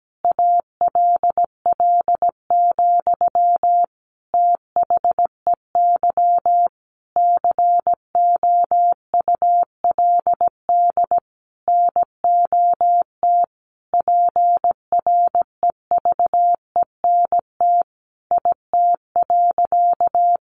Morse code, heard as a keyed tone, about 17 wpm, 700 Hz. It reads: ALL, THEY COULD NOT PREVENT IT.
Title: Sentences from Top 1100 Words 17wpm